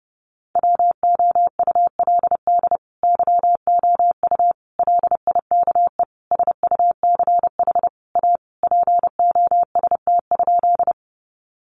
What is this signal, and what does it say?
Morse code, heard as a keyed tone, about 30 wpm, 700 Hz. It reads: WOULD YOU LIKE SUCH A POST?
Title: Sentences from Top 1000 Words 30wpm